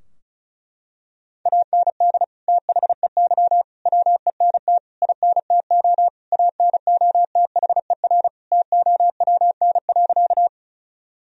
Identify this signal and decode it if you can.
Morse code, heard as a keyed tone, about 36 wpm, 700 Hz. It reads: AND THEY WENT INTO ANOTHER TOWN.
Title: Sentences from Top 1000 Words - Review Part 2 - 35wpm